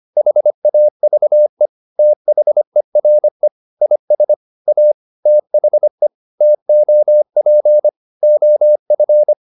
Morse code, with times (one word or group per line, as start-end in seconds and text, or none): 0.2-1.7 HAVE
2.0-3.5 THERE
3.8-4.3 IS
4.7-4.9 A
5.3-6.1 THE
6.4-7.9 TOP
8.2-9.3 OF